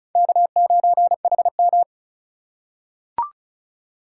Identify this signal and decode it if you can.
Morse code, heard as a keyed tone, about 36 wpm, 700 Hz. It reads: K9HM
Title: USA Call Signs - 3x Character Spacing - 35wpm